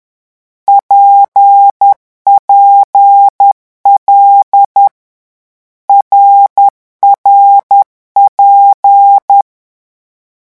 Morse code, heard as a keyed tone, about 10 wpm, 800 Hz. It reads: PPL RRP
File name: morse mp3 / lecon6-vitesse 10